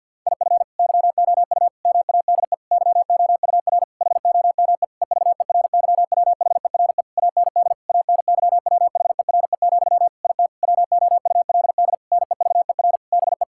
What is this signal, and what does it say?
Morse code, heard as a keyed tone, about 50 wpm, 700 Hz. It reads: IF YOU MADE YOUR HOME EVERYWHERE AND ANYWHERE, IT WOULD NEVER BE
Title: Sentences from Top 1600 Words 50wpm